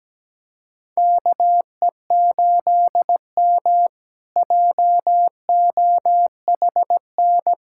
Morse code, heard as a keyed tone, about 17 wpm, 700 Hz. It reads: KE8M JOHN
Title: K1USN SST QSO Practice - 17wpm